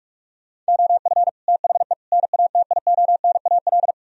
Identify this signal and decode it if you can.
Morse code, heard as a keyed tone, about 58 wpm, 700 Hz. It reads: OF THE NATIONAL